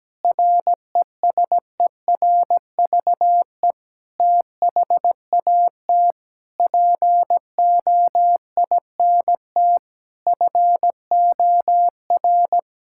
Morse code, heard as a keyed tone, about 17 wpm, 700 Hz. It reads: RESERVE THAT POINT FOR
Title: Sentences from Top 1000 Words - Review Part 6 - 17wpm